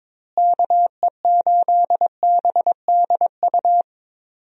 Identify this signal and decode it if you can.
Morse code, heard as a keyed tone, about 22 wpm, 700 Hz. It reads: KE8BDU
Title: USA Call Signs - 6 Characters - Repeated 3x - 22wpm